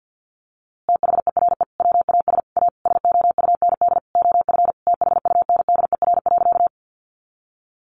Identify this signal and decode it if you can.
Morse code, heard as a keyed tone, about 50 wpm, 700 Hz. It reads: THERE WAS A SOUND OF THUNDER.
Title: A Sound Of Thunder 50wpm